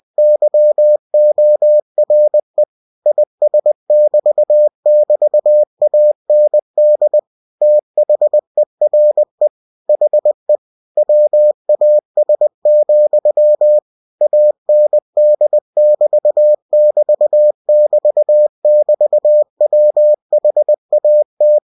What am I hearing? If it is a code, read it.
Morse code, heard as a keyed tone, about 20 wpm, 600 Hz. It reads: YORE IS<BT><BT>AND THERE HE WAS, AND<BT><BT><BT><BT>WHAT
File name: cw-books / winnie-the-pooh / winnie-20wpm-page-0048